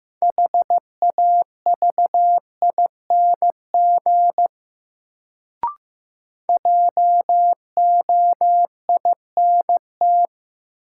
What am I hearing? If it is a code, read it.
Morse code, heard as a keyed tone, about 15 wpm, 700 Hz. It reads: HAVING JOINT
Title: Sets of 5 Words Encore - New Word Drill - 15wpm